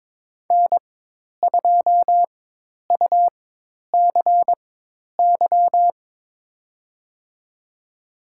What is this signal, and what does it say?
Morse code, heard as a keyed tone, about 22 wpm, spaced as for 10 wpm, 700 Hz. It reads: N2UCY
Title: USA Call Signs - 5 Characters - 4x Character Spacing - 22wpm